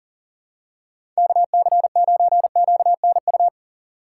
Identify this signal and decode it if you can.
Morse code, heard as a keyed tone, about 40 wpm, 700 Hz. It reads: KC9QNU
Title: USA Call Signs - 8x Character Spacing - 40wpm